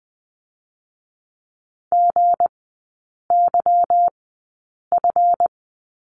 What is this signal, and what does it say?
Morse code, heard as a keyed tone, about 19 wpm, spaced as for 8 wpm, 700 Hz. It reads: GYF
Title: 159 Lesson 26e - Letter g - Three Characters 20wpm